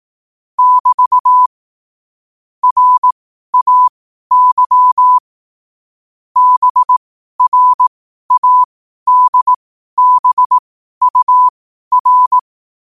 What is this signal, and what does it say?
Morse code, heard as a keyed tone, about 18 wpm, spaced as for 12 wpm, 1000 Hz. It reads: <BT> RAY BRADBUR
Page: New quotes every day in morse code at 12 Words per minute.